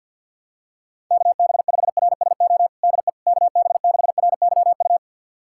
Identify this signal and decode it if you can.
Morse code, heard as a keyed tone, about 51 wpm, 700 Hz. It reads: KB5RSO DE KB6RYU
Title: USA Callsign DE Callsign - 6 Characters - 50wpm